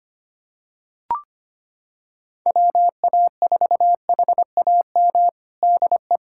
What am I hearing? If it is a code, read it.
Morse code, heard as a keyed tone, about 25 wpm, 700 Hz. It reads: WA4HAM DE
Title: USA Callsign DE Callsign - 6 Characters - 25wpm